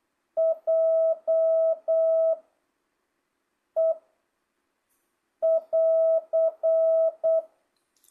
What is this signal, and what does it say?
Morse code, heard as a keyed tone, about 8 wpm, 660 Hz. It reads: J E <AR>
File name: jh8cbh / 2025cw